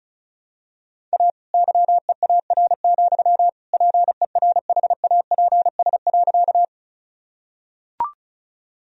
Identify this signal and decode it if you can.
Morse code, heard as a keyed tone, about 36 wpm, 700 Hz. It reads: A YEAR, PERHAPS.
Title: Sentences from Top 1000 Words - Review Part 1 - 35wpm